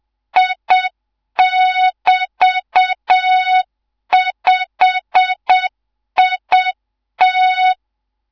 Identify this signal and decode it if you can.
Morse code, heard as a keyed tone, about 7 wpm, 3700 Hz, with noise. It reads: I<BT>5IT